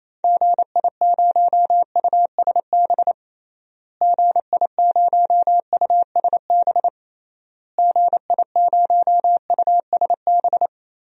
Morse code, heard as a keyed tone, about 28 wpm, 700 Hz. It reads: GI0USB GI0USB GI0USB
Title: DXCC-100 Call Signs - 6 Characters - Repeated 3x - 28wpm